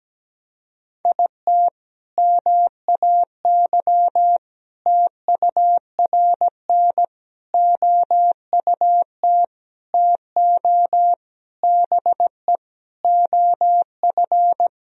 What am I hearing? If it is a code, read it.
Morse code, heard as a keyed tone, about 17 wpm, 700 Hz. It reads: IT MAY TURN OUT TO BE OF
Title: Sentences from Top 400 Words and Abbreviations 17wpm